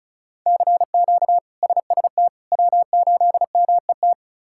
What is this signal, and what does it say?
Morse code, heard as a keyed tone, about 36 wpm, 700 Hz. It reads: CQ SST W8MET
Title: K1USN SST QSO Practice - 35wpm